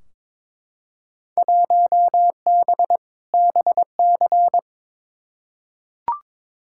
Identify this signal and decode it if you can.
Morse code, heard as a keyed tone, about 22 wpm, 700 Hz. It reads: 1B BC
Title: Field Day Exchanges - 22wpm